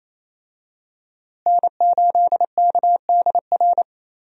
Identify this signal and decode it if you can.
Morse code, heard as a keyed tone, about 28 wpm, 700 Hz. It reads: N8KDR
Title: USA Call Signs - 5 Characters - Repeated 3x - 28wpm